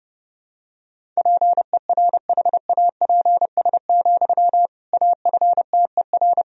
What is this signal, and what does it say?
Morse code, heard as a keyed tone, about 30 wpm, 700 Hz. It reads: PERHAPS, AFTER
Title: Sentences from Top 500 Words and Abbreviations 30wpm